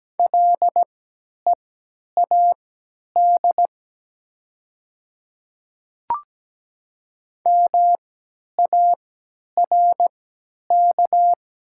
Morse code, heard as a keyed tone, about 17 wpm, spaced as for 10 wpm, 700 Hz. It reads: LEAD MARK
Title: 3x Character Spacing - Top 500 Words - 17wpm